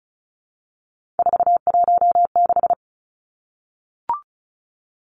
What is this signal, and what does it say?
Morse code, heard as a keyed tone, about 34 wpm, 700 Hz. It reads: 416
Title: Numbers - 3-Digit 35wpm